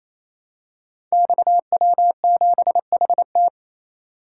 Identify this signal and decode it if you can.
Morse code, heard as a keyed tone, about 27 wpm, 700 Hz. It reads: XW7HT
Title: Code Groups - 5 Letter:Numbers - 28wpm